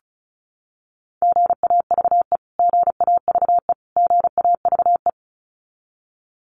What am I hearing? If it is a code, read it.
Morse code, heard as a keyed tone, about 34 wpm, 700 Hz. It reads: GAVE GAVE GAVE
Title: Top 500 Words Repeated 35wpm